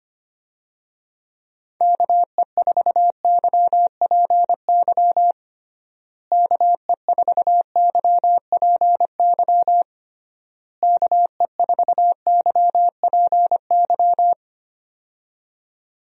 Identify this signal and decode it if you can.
Morse code, heard as a keyed tone, about 25 wpm, 700 Hz. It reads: KE4YPY KE4YPY KE4YPY
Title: USA Call Signs - 6 Characters - Repeated 3x - 25wpm